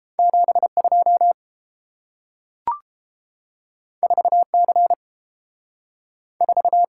Morse code, heard as a keyed tone, about 30 wpm, 700 Hz. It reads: 7 2 4 C 4